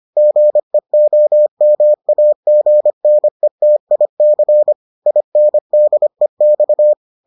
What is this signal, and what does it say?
Morse code, heard as a keyed tone, about 25 wpm, 600 Hz. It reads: GEOMAGNETIC INDEX